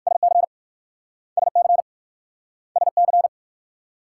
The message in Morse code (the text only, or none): SC SC SC